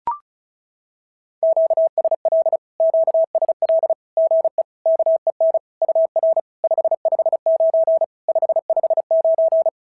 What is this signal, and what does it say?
Morse code, heard as a keyed tone, about 36 wpm, 650 Hz. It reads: QSL QSL GE KEN UR 559 559
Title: SOTA QSO Practice 35wpm